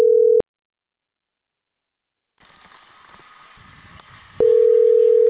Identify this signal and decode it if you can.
Ringback tone (North America), 440 + 480 Hz.